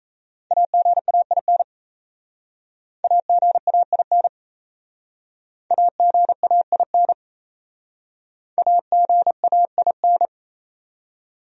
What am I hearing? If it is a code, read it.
Morse code, heard as a keyed tone, about 36 wpm, 700 Hz. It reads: AGAIN AGAIN AGAIN AGAIN